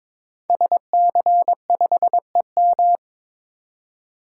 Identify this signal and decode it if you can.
Morse code, heard as a keyed tone, about 22 wpm, 700 Hz. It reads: SC5EM
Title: Code Groups - 5 Letter:Numbers - 22wpm